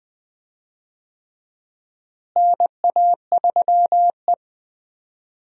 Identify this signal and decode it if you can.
Morse code, heard as a keyed tone, about 20 wpm, 700 Hz. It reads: NA3E
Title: USA Call Signs - 4 Characters - Repeated 3x - 20wpm